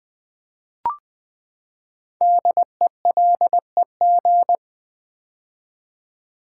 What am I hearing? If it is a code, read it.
Morse code, heard as a keyed tone, about 20 wpm, 700 Hz. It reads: DELEG
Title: Build A Word - 5 Letters At A Time - 10-Letter Words - 20wpm